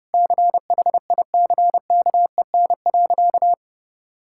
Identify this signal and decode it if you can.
Morse code, heard as a keyed tone, about 30 wpm, 700 Hz. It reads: CHICKEN.
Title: Sentences from Top 1000 Words - Review Part 5 - 30wpm